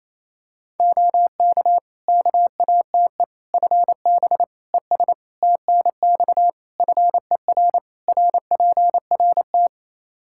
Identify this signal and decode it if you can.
Morse code, heard as a keyed tone, about 28 wpm, 700 Hz. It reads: OK KATE FB ES TNX FER RPRT
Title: Practice QSO - Exchange 2 - Info - 28wpm